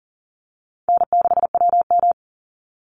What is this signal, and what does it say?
Morse code, heard as a keyed tone, about 40 wpm, 700 Hz. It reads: N6WM
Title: Easy Call Signs - 40wpm